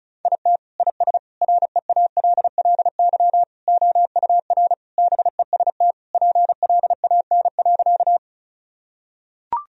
Morse code, heard as a keyed tone, about 36 wpm, 700 Hz. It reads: IT IS REALLY OUR BEST PLAN.
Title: Sentences from Top 1000 Words - Review Part 2 - 35wpm